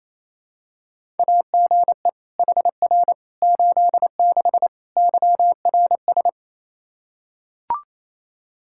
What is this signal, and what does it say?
Morse code, heard as a keyed tone, about 28 wpm, 700 Hz. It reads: AGE HR 86 YRS
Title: Practice QSO - Exchange 3 - More Info - 28wpm